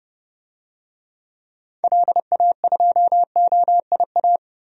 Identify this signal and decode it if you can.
Morse code, heard as a keyed tone, about 30 wpm, 700 Hz. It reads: LA2OIA
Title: DXCC-100 Call Signs - 6 Characters - Repeated 2x - 30wpm